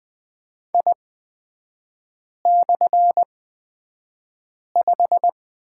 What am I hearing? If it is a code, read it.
Morse code, heard as a keyed tone, about 20 wpm, spaced as for 5 wpm, 700 Hz. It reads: I/5
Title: 236 Lesson 38e - Letter q - Three Characters 20wpm